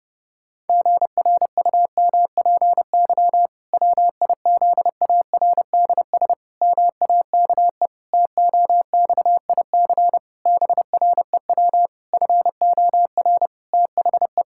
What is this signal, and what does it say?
Morse code, heard as a keyed tone, about 30 wpm, 700 Hz. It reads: GRUMPY WIZARDS MAKE TOXIC BREW FOR THE
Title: Pangram Challenge v2 - 30wpm